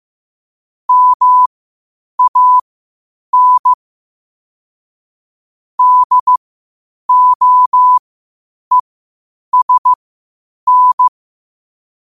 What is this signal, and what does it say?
Morse code, heard as a keyed tone, about 15 wpm, spaced as for 8 wpm, 1000 Hz. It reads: MAN DOESN
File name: qotdfiles / 8wpm